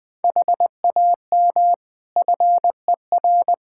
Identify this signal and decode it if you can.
Morse code, heard as a keyed tone, about 21 wpm, 700 Hz. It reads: HAM FER
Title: K1USN SST QSO Practice 20wpm